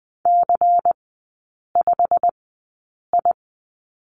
Morse code, heard as a keyed tone, about 20 wpm, spaced as for 8 wpm, 690 Hz. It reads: C5I
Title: 093 Lesson 16e - Letter c - Three Characters 20wpm